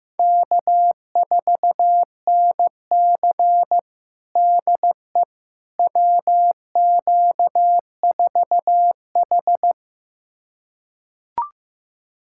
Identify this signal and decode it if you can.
Morse code, heard as a keyed tone, about 16 wpm, 700 Hz. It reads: K4NC DE WQ4H
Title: USA Callsign DE Callsign - 4 Characters - 15wpm